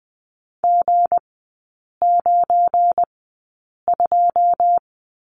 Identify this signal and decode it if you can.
Morse code, heard as a keyed tone, about 20 wpm, spaced as for 8 wpm, 700 Hz. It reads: G92
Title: 184 Lesson 30e - Letter b - Three Characters 20wpm